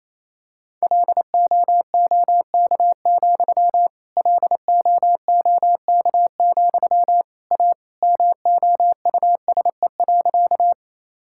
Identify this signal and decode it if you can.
Morse code, heard as a keyed tone, about 28 wpm, 700 Hz. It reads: LOOK, LOOK, A MOUSE.
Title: Sentences from Top 1000 Words - Review Part 5 - 28wpm